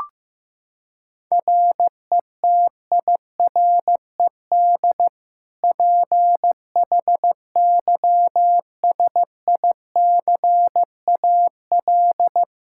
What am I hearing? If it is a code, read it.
Morse code, heard as a keyed tone, about 15 wpm, 700 Hz. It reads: RETIRED PHYSICAL
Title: K1USN SST QSO Practice 15wpm